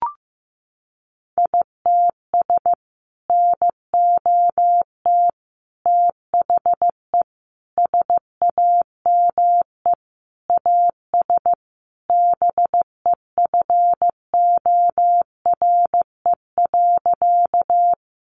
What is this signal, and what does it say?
Morse code, heard as a keyed tone, about 15 wpm, 700 Hz. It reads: ITS NOT THE SAME AS BEFORE.
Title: Sentences from Top 600 Words 15wpm